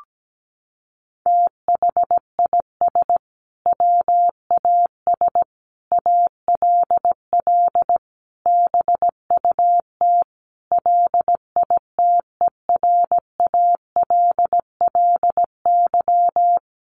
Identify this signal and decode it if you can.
Morse code, heard as a keyed tone, about 17 wpm, 700 Hz. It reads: THIS WAS ALL BUT LITERALLY
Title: Sentences from Top 1400 Words 17wpm